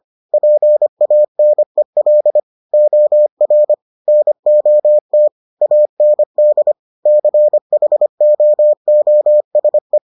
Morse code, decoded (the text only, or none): PANEL OR NOT AND CHOOSE